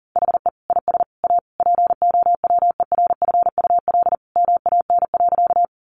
Morse code, heard as a keyed tone, about 38 wpm, 700 Hz. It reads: HE IS A POWERFUL MAN.
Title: Sentences from Top 1300 Words 40wpm